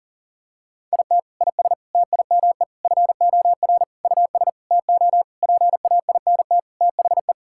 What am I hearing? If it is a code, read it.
Morse code, heard as a keyed tone, about 40 wpm, 700 Hz. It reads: IT IS TIME FOR US TO PAINT THE
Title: Sentences from Top 1000 Words - Review Part 6 - 40wpm